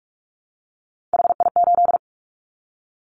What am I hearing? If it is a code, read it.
Morse code, heard as a keyed tone, about 44 wpm, 700 Hz. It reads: HI8
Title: Callsign Prefixes for Top 100 DXCC - 45wpm